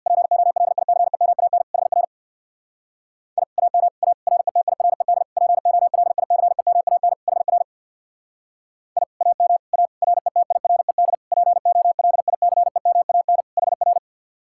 Morse code, heard as a keyed tone, about 64 wpm, 700 Hz. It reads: POLICEMAN HR I AM A RETIRED POLICEMAN HR I AM A RETIRED POLICEMAN HR